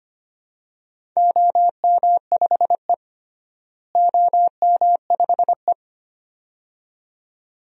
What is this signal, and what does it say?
Morse code, heard as a keyed tone, about 25 wpm, 700 Hz. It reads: OM5E OM5E
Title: DXCC-100 Call Signs - 4 Characters - Repeated 2x - 25wpm